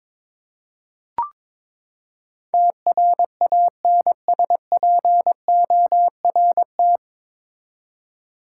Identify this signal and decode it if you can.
Morse code, heard as a keyed tone, about 22 wpm, 700 Hz. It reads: TRANSPORT
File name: practice / 22wpm / Nine Letter Words - 22wpm